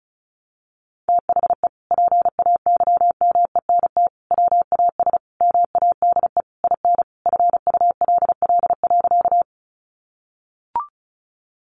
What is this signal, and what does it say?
Morse code, heard as a keyed tone, about 34 wpm, 700 Hz. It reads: THE PAYMENT WAS MADE IN FULL.
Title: Sentences from Top 1200 Words 35wpm